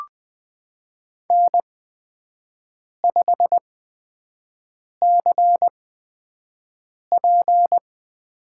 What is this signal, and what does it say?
Morse code, heard as a keyed tone, about 20 wpm, spaced as for 5 wpm, 700 Hz. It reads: N5CP
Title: USA Call Signs - 8x Character Spacing - 20wpm